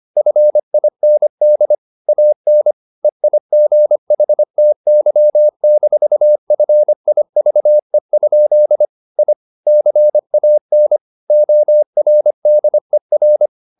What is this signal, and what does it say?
Morse code, heard as a keyed tone, about 25 wpm, 600 Hz. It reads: FIND AN EIGHTY-FIVE? I CAN ORDER
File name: the old man and the sea part 1 25wpm